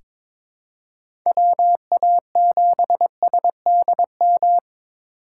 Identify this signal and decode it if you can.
Morse code, heard as a keyed tone, about 22 wpm, 700 Hz. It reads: WA7SDM
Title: USA Call Signs - 6 Characters - 4x Character Spacing - 22wpm